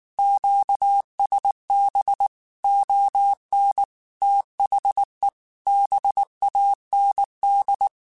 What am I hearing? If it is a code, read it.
Morse code, heard as a keyed tone, about 19 wpm, 800 Hz. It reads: QSB ON THE BAND
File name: QSOs / QSOs-mp3s / WD0BGP 1@18